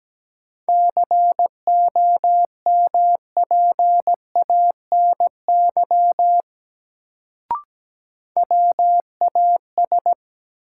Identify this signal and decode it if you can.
Morse code, heard as a keyed tone, about 17 wpm, 700 Hz. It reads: COMPANY WAS
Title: Sets of 2 Words - New Word Drill - 17wpm